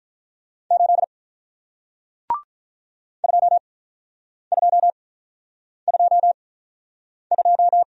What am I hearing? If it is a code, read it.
Morse code, heard as a keyed tone, about 47 wpm, spaced as for 9 wpm, 700 Hz. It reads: 8 2222